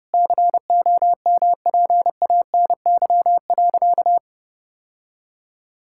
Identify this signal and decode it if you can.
Morse code, heard as a keyed tone, about 30 wpm, 700 Hz. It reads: COMPANY.
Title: Sentences from Top 1000 Words - Review Part 5 - 30wpm